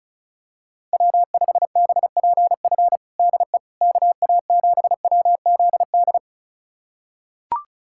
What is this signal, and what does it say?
Morse code, heard as a keyed tone, about 34 wpm, 690 Hz. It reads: W5BPF DE KA7WZD
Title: USA Callsign DE Callsign 35wpm